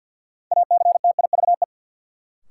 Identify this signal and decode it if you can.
Morse code, heard as a keyed tone, about 51 wpm, 700 Hz. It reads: ACTIVE
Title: Sentences from Top 900 Words - New Word Drill - 50wpm